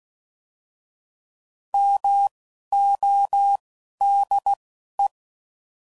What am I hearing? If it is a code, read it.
Morse code, heard as a keyed tone, about 16 wpm, spaced as for 11 wpm, 800 Hz. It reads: MODE